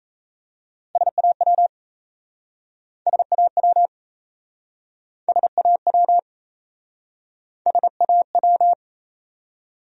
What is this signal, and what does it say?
Morse code, heard as a keyed tone, about 38 wpm, 700 Hz. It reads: SAW SAW SAW SAW